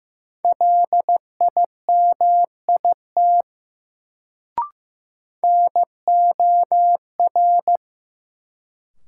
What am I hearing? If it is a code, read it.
Morse code, heard as a keyed tone, about 15 wpm, 700 Hz. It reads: LIMIT NOR
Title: Sentences from Top 1500 Words - New Word Drill Alt - 15wpm